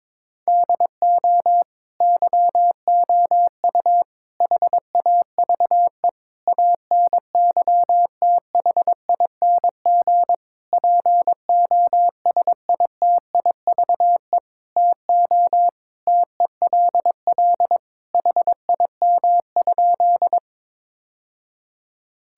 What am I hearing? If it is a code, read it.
Morse code, heard as a keyed tone, about 22 wpm, 700 Hz. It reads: DO YOU HAVE ANYTHING POSITIVE TO TELL HIM?
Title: Sentences from Top 1000 Words - Review Part 5 - 22wpm